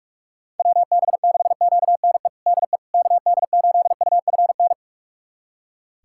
Morse code, heard as a keyed tone, about 46 wpm, 700 Hz. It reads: WB6QNE DE KD8UFN
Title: USA Callsign DE Callsign - 6 Characters - 45wpm